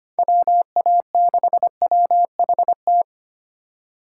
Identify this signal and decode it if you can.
Morse code, heard as a keyed tone, about 25 wpm, 700 Hz. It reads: WA6WHT
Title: USA Call Signs - 6 Characters - 4x Character Spacing - 25wpm